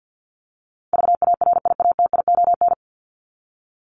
Morse code, heard as a keyed tone, about 50 wpm, 700 Hz. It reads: VARIATION
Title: Top 1501-1750 Words 50wpm